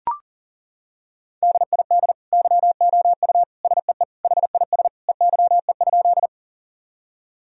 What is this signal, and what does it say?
Morse code, heard as a keyed tone, about 40 wpm, 700 Hz. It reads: DID YOU SEE HIS EYE?
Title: Sentences from Top 1000 Words - Review Part 1 - 40wpm